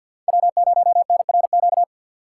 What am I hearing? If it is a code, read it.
Morse code, heard as a keyed tone, about 53 wpm, 700 Hz. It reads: W0NRQ
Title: USA Call Signs - 5 Characters - 3x Character Spacing - 50wpm